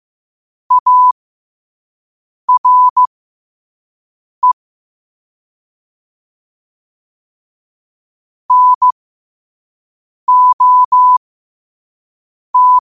Morse code, heard as a keyed tone, about 15 wpm, spaced as for 5 wpm, 1000 Hz. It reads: ARE NOT
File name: qotdfiles / 5wpm